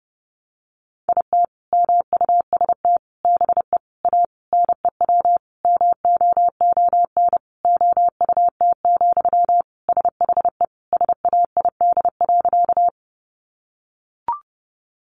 Morse code, heard as a keyed tone, about 30 wpm, 700 Hz. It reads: IT MUST BE A NEW MOON OUT, SHE SAID.
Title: Sentences from Top 500 Words 30wpm